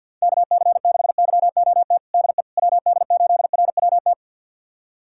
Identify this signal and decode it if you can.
Morse code, heard as a keyed tone, about 47 wpm, 690 Hz. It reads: KC6YOT DE WD8RWT
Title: USA Callsign DE Callsign 50wpm